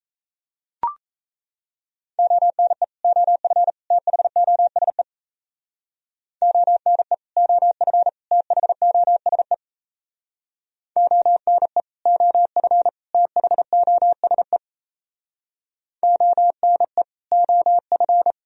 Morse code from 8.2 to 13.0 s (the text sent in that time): THOSE ONE OF